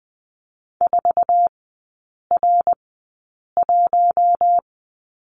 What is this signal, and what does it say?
Morse code, heard as a keyed tone, about 20 wpm, spaced as for 8 wpm, 690 Hz. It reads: 4R1
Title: 087 Lesson 15e - Number 5 - Three Characters 20wpm